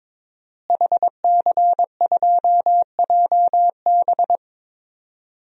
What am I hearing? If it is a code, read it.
Morse code, heard as a keyed tone, about 22 wpm, 700 Hz. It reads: HC2JB
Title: DXCC-100 Call Signs - 5 Characters - Repeated 2x - 22wpm